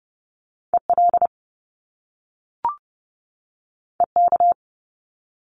Morse code, heard as a keyed tone, about 30 wpm, 700 Hz. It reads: EL EK
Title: Top Two Letter Combinations in Words 30wpm